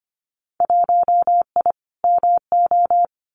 Morse code, heard as a keyed tone, about 25 wpm, 700 Hz. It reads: 1I MO
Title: Winter Field Day Exchanges Encore 25wpm